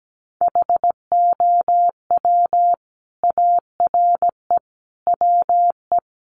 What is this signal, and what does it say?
Morse code, heard as a keyed tone, about 17 wpm, 700 Hz. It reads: HOW ARE WE
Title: Sets of 5 Words Encore 17wpm